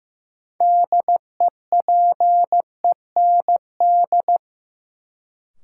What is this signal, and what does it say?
Morse code, heard as a keyed tone, about 15 wpm, 700 Hz. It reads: DEPEND
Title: Sentences from Top 1300 Words - New Word Drill Alt - 15wpm